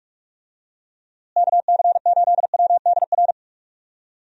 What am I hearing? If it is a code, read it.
Morse code, heard as a keyed tone, about 46 wpm, 700 Hz. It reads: KC8WDR
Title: USA Call Signs - 6 Characters - Repeated 2x - 45wpm